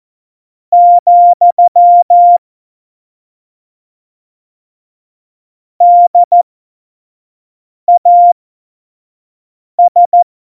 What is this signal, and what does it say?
Morse code, heard as a keyed tone, about 14 wpm, spaced as for 5 wpm, 700 Hz. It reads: , DAS
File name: Morsekurs / Witze / Witz 22. Mai 2018, 25 F70